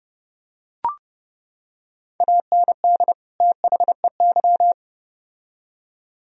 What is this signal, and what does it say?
Morse code, heard as a keyed tone, about 30 wpm, 700 Hz. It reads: AND THEY